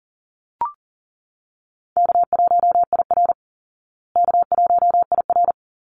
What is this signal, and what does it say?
Morse code, heard as a keyed tone, about 40 wpm, 700 Hz. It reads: K1IR K1IR
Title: Call Signs Repeated 3x - 40wpm